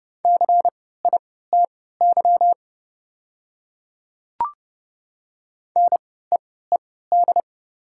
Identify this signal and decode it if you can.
Morse code, heard as a keyed tone, about 30 wpm, spaced as for 17 wpm, 700 Hz. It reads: CITY NEED
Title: 3x Character Spacing - Top 300 Words - 30wpm